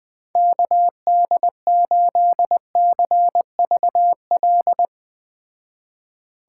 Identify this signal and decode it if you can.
Morse code, heard as a keyed tone, about 21 wpm, 700 Hz. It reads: KD8CVL